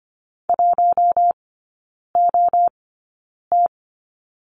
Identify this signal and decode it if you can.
Morse code, heard as a keyed tone, about 25 wpm, spaced as for 9 wpm, 700 Hz. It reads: 1OT
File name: Lesson 15e - Number 5 - Three Characters 25wpm